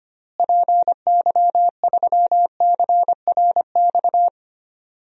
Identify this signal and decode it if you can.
Morse code, heard as a keyed tone, about 25 wpm, 700 Hz. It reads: PY3CRX
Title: DXCC-100 Call Signs - 6 Characters - Repeated 3x - 25wpm